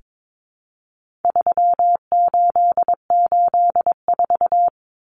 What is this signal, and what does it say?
Morse code, heard as a keyed tone, about 22 wpm, 700 Hz. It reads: 3884
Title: Common Cut Numbers - 4-Digit 22wpm